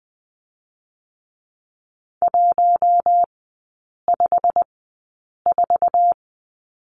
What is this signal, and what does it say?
Morse code, heard as a keyed tone, about 20 wpm, spaced as for 8 wpm, 700 Hz. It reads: 154
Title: 205 Lesson 33e - Letter j - Three Characters 20wpm